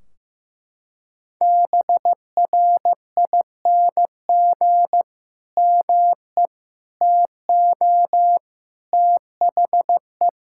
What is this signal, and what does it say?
Morse code, heard as a keyed tone, about 15 wpm, 700 Hz. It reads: BRING ME TO THE
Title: Sentences from Top 1600 Words 15wpm